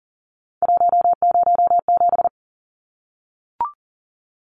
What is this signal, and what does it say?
Morse code, heard as a keyed tone, about 40 wpm, 700 Hz. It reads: 107
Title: Numbers - 3-Digit 40wpm